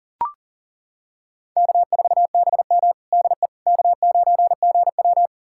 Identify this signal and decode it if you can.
Morse code, heard as a keyed tone, about 37 wpm, 700 Hz. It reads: K4BM DE K9GW
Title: USA Callsign DE Callsign - 4 Characters - 40wpm